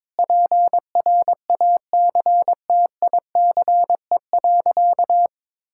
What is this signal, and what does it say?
Morse code, heard as a keyed tone, about 22 wpm, 700 Hz. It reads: PRACTICE.
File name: practice / 22wpm / Sentences with 8-Word Phrases - 22wpm